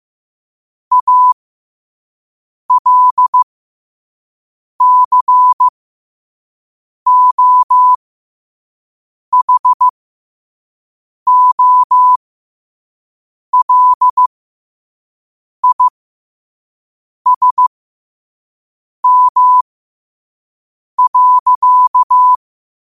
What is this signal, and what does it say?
Morse code, heard as a keyed tone, about 15 wpm, spaced as for 5 wpm, 1000 Hz. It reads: ALCOHOLISM.
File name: qotdfiles / 5wpm